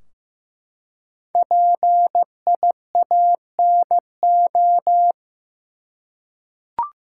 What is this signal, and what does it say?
Morse code, heard as a keyed tone, about 15 wpm, 700 Hz. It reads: PIANO
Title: Five Letter Words - 15wpm